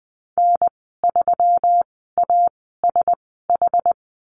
Morse code, heard as a keyed tone, about 20 wpm, spaced as for 14 wpm, 700 Hz. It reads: N3ASH
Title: 238 Lesson 38g - Letter q - Call Signs 20wpm